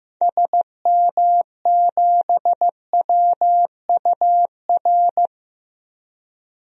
Morse code, heard as a keyed tone, about 15 wpm, 700 Hz. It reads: SM7WUR
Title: DXCC-100 Call Signs - 6 Characters - 15wpm